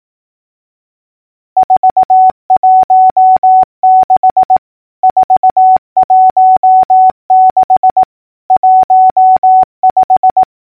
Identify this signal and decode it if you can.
Morse code, heard as a keyed tone, about 18 wpm, 750 Hz. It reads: 416 416 15
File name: numbers / V2-morse-clicks